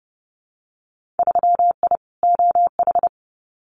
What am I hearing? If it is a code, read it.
Morse code, heard as a keyed tone, about 30 wpm, 700 Hz. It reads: 3I OH
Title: Winter Field Day Exchanges Encore 30wpm